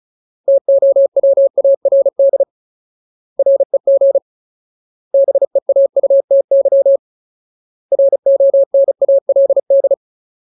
Morse code, heard as a keyed tone, about 34 wpm, 550 Hz. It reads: TOWARD REG BEAUTY RONALD
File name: Ultimate Word List / 35WPM Ultimate